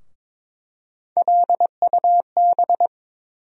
Morse code, heard as a keyed tone, about 22 wpm, 700 Hz. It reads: LUB
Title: USA Call Sign Postfixes - 3 Letters 22wpm